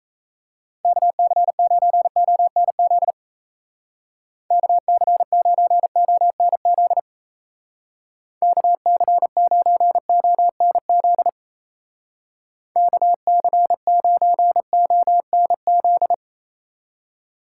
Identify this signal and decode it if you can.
Morse code, heard as a keyed tone, about 35 wpm, 700 Hz. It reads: KC9ONZ KC9ONZ KC9ONZ KC9ONZ
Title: Speed Racing - USA 2x3 Call Signs 28wpm